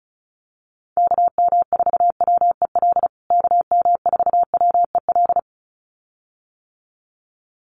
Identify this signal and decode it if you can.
Morse code, heard as a keyed tone, about 34 wpm, 700 Hz. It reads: KM4WEL KM4WEL
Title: USA Call Signs Repeated 2x 35wpm